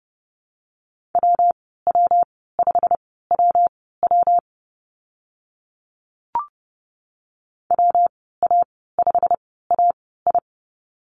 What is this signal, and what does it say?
Morse code, heard as a keyed tone, about 30 wpm, spaced as for 17 wpm, 700 Hz. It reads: WW5WW WA5AI
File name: Lesson 19g - Letter w - Call Signs 30wpm